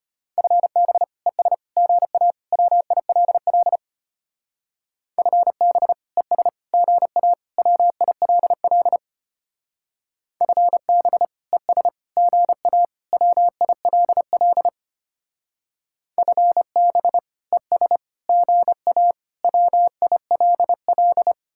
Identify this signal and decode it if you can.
Morse code, heard as a keyed tone, about 32 wpm, 700 Hz. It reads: FB ES GA WILL FB ES GA WILL FB ES GA WILL FB ES GA WILL